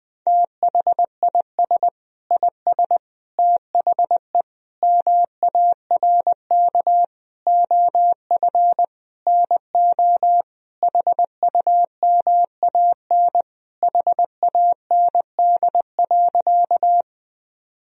Morse code, heard as a keyed tone, about 20 wpm, 700 Hz. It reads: THIS IS THE MARK OF NO HUMAN HAND.
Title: Sentences from Top 1000 Words - Review Part 3 - 20wpm